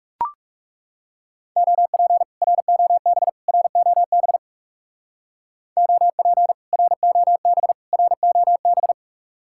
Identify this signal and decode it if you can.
Morse code, heard as a keyed tone, about 37 wpm, 700 Hz. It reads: OP ROB ROB OP ROB ROB
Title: Speed-Racing Common QSO Phrases Encore - 30wpm